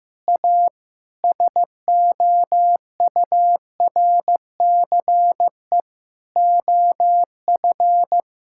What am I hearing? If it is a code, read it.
Morse code, heard as a keyed tone, about 16 wpm, 700 Hz. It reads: A SOURCE OF
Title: Sentences from Top 1000 Words - Review Part 5 - 15wpm